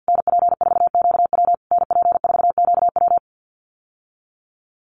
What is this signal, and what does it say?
Morse code, heard as a keyed tone, about 50 wpm, 700 Hz. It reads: NP4QW NP4QW
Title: USA Call Signs Repeated 3x 50wpm